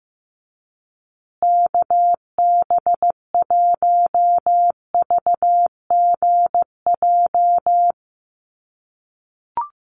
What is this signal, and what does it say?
Morse code, heard as a keyed tone, about 15 wpm, 700 Hz. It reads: KB1VGJ
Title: USA Call Signs Repeated 2x 15wpm